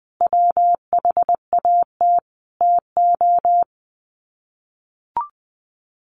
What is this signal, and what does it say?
Morse code, heard as a keyed tone, about 19 wpm, 700 Hz. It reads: WHAT TO
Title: Sets of 4 Words 20wpm